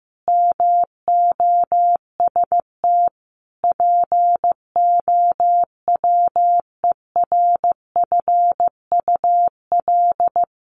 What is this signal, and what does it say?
Morse code, heard as a keyed tone, about 15 wpm, 700 Hz. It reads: MOST POWERFUL
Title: Sets of 5 Words Encore 15wpm